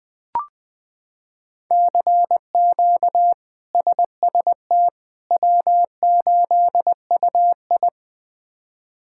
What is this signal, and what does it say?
Morse code, heard as a keyed tone, about 20 wpm, 700 Hz. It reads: CQ SST W8UI
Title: K1USN SST QSO Practice - 20wpm